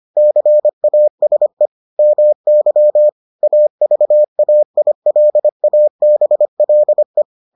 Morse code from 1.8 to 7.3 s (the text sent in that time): MY AVAILABLE